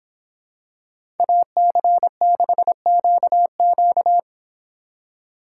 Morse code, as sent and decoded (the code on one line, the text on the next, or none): .- -.-. -.... --.- --.-
AC6QQ